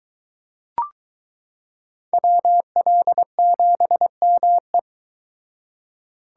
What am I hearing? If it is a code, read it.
Morse code, heard as a keyed tone, about 22 wpm, 700 Hz. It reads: WL7ME